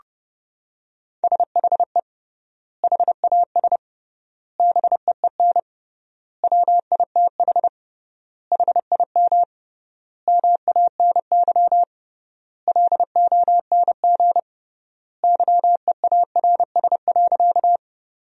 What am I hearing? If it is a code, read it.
Morse code, heard as a keyed tone, about 30 wpm, 700 Hz. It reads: SHE HAS BEEN WITH HIM MANY LONG YEARS.
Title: 3x Word-Spacing - Sentences from Top 200 Words - 30wpm